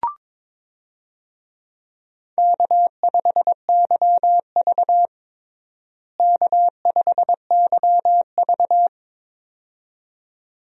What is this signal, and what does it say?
Morse code, heard as a keyed tone, about 22 wpm, 700 Hz. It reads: K5YV K5YV
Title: USA Call Signs - 4 Characters - Repeated 2x - 22wpm